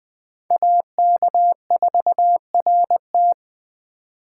Morse code, heard as a keyed tone, about 20 wpm, 700 Hz. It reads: AK4RT
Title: USA Call Signs - 2x Character Spacing - 20wpm